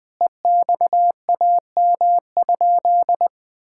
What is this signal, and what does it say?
Morse code, heard as a keyed tone, about 20 wpm, 700 Hz. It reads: EXAM?
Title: Sentences from Top 1000 Words - Review Part 5 - 20wpm